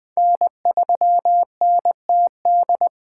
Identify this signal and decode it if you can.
Morse code, heard as a keyed tone, about 20 wpm, 700 Hz. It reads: N3NTD
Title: Speed Racing - USA 1x3 Call Signs 17wpm